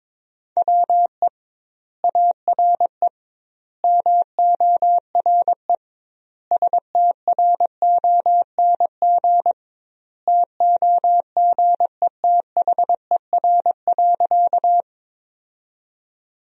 Morse code, heard as a keyed tone, about 21 wpm, 700 Hz. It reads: WE ARE MORE STRONG TOGETHER.
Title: 2x Word-Spacing - Sentences from Top 500 Words - 22wpm